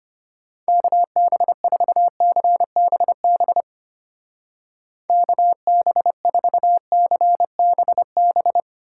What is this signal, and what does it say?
Morse code, heard as a keyed tone, about 32 wpm, 700 Hz. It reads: KB4CBB KB4CBB